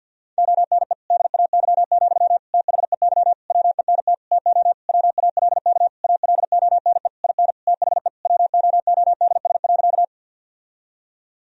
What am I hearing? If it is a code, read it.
Morse code, heard as a keyed tone, about 50 wpm, 700 Hz. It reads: ONE DAY, THEY WENT TO WALK ALONE IN THE WOODS.
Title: Sentences from Top 1000 Words - Review Part 2 - 50wpm